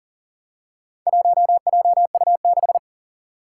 Morse code, heard as a keyed tone, about 40 wpm, 700 Hz. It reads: 1JU6
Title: Four Challenging Letter-Number Combinations 40wpm